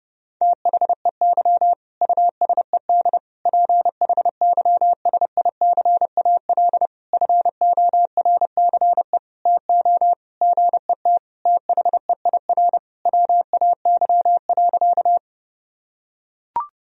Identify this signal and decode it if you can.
Morse code, heard as a keyed tone, about 30 wpm, 700 Hz. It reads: THEY USED PHYSICAL FORCE TO GET THEIR WAY.
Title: Sentences from Top 1000 Words - Review Part 4 - 30wpm